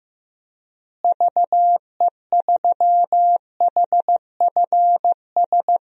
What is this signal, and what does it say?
Morse code, heard as a keyed tone, about 15 wpm, 700 Hz. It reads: VE3HFS
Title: DXCC-100 Call Signs - 6 Characters - Repeated 3x - 15wpm